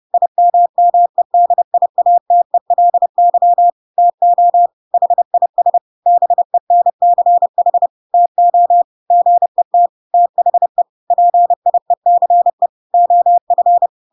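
Morse code, heard as a keyed tone, about 30 wpm, 700 Hz. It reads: IMMEDIATELY TO HIS BENCH TO GET THE PIECE OF
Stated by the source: Chapter 2 of The Adventures of Pinocchio read as a book in Morse code (CW) at 30 wpm, by ik1hge.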